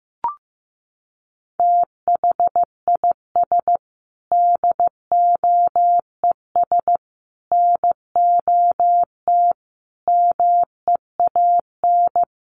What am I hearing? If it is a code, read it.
Morse code, heard as a keyed tone, about 15 wpm, 700 Hz. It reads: THIS DOES NOT MEAN
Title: Sets of 5 Words Encore2 15wpm